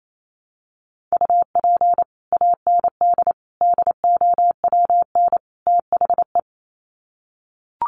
Morse code, heard as a keyed tone, about 28 wpm, 700 Hz. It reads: UP AND DOWN THE
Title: Sets of 4 Words 28wpm